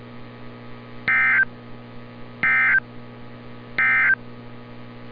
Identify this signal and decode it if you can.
Emergency Alert System (SAME) tones, heard as harsh buzzing bursts. It is heard from 1.0 to 4.2 s.
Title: NOAA Weather Radio - Altoona, PA